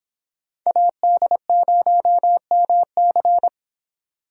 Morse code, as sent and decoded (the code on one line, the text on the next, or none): .- -.. ----- -- -.-.
AD0MC